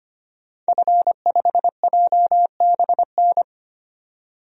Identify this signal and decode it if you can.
Morse code, heard as a keyed tone, about 25 wpm, 700 Hz. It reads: F5JBN
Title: DXCC-100 Call Signs - 5 Characters - 25wpm